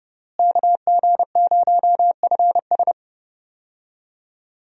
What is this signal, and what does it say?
Morse code, heard as a keyed tone, about 30 wpm, 700 Hz. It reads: KG0FS
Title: USA Call Signs - 5 Characters - Repeated 2x - 30wpm